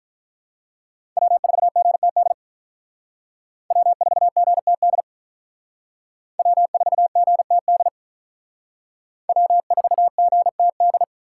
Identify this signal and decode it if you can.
Morse code, heard as a keyed tone, about 43 wpm, 700 Hz. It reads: W4GTD W4GTD W4GTD W4GTD
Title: Speed Racing - USA 1x3 Call Signs 35wpm